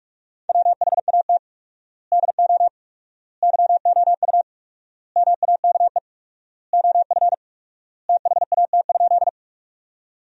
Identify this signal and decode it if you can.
Morse code, heard as a keyed tone, about 46 wpm, 700 Hz. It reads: WHAT DO YOU MAKE OF THAT?
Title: 4x Word-Spacing - Sentences from Top 100 Words - 45wpm